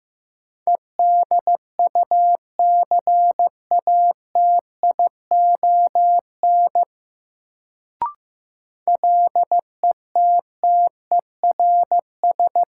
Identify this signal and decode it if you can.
Morse code, heard as a keyed tone, about 15 wpm, 700 Hz. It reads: EDUCATION LETTERS
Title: Sentences from Top 1000 Words - Review Part 6 - New Word Drill - 15wpm